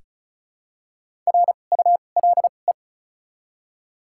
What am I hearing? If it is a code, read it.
Morse code, heard as a keyed tone, about 34 wpm, spaced as for 25 wpm, 700 Hz. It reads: RULE
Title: 2x Character Spacing - Top 500 Words - 35wpm